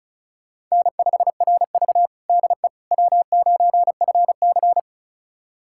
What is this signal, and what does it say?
Morse code, heard as a keyed tone, about 36 wpm, 700 Hz. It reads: N5RV DE W9FC
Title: USA Callsign DE Callsign - 4 Characters - 35wpm